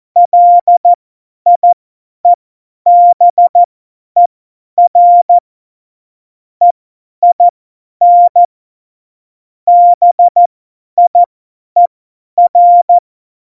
Morse code, heard as a keyed tone, about 14 wpm, spaced as for 10 wpm, 700 Hz. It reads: LIEBER EIN BIER
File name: Morsekurs / Witze / Witz 01. Oktober 2019, 50 F70